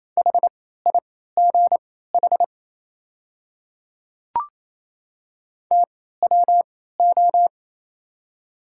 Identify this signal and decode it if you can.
Morse code, heard as a keyed tone, about 28 wpm, spaced as for 16 wpm, 700 Hz. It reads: HIGH TWO
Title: 3x Character Spacing - Top 300 Words - 28wpm